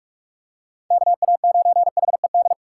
Morse code, heard as a keyed tone, about 46 wpm, 700 Hz. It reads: KA9HED